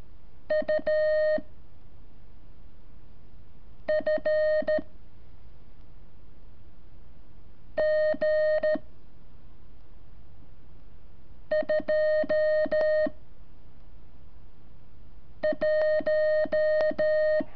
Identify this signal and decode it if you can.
Morse code, heard as a keyed tone, about 13 wpm, 620 Hz. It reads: U F G 2 1
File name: curso / ejercicio 26